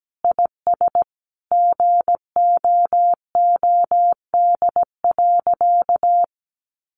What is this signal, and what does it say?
Morse code, heard as a keyed tone, about 17 wpm, 700 Hz. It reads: IS GOOD.
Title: Sentences from Top 800 Words 17wpm